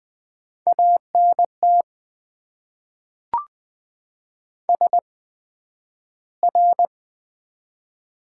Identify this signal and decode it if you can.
Morse code, heard as a keyed tone, about 20 wpm, spaced as for 11 wpm, 700 Hz. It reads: ANT SR
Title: Top 90 QSO Elements - 8x Character Spacing - 20wpm